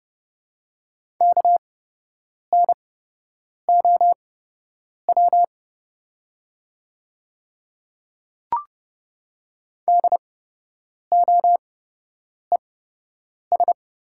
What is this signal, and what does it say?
Morse code, heard as a keyed tone, about 29 wpm, spaced as for 8 wpm, 700 Hz. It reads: KNOW DOES
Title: 8x Character Spacing - Top 200 Words - 30wpm